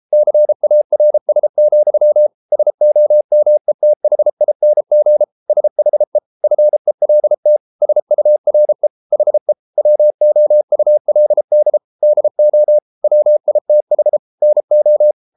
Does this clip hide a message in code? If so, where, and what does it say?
Yes: Morse code from 0.1 to 15.1 s, heard as a keyed tone, about 33 wpm, 600 Hz. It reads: CARS, SOMETHING SHE FELT SURE HE WOULD DO WITH NO